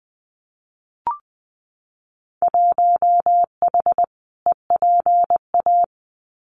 Morse code, heard as a keyed tone, about 20 wpm, 700 Hz. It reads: 1H EPA
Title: Winter Field Day Exchanges 20wpm